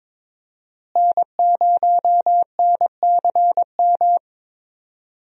Morse code, heard as a keyed tone, about 22 wpm, 700 Hz. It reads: N0NCM
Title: USA Call Signs - 5 Characters - Repeated 3x - 22wpm